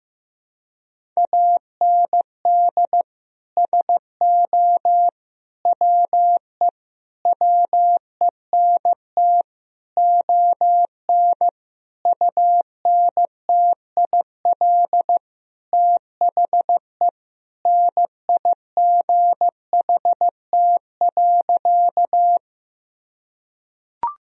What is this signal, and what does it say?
Morse code, heard as a keyed tone, about 15 wpm, 700 Hz. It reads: AND SO WE WENT ON UNTIL THE NIGHT.
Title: Sentences from Top 1000 Words - Review Part 1 - 15wpm